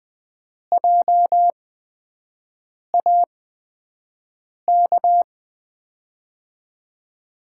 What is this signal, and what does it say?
Morse code, heard as a keyed tone, about 20 wpm, spaced as for 5 wpm, 700 Hz. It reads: JAK
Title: USA Call Signs - 6 Characters - 8x Character Spacing - 20wpm